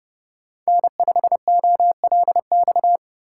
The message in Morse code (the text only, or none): N5OLX